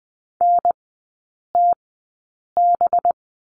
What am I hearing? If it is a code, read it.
Morse code, heard as a keyed tone, about 20 wpm, spaced as for 8 wpm, 690 Hz. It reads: NTB
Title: 212 Lesson 34e - Number 8 - Three Characters 20wpm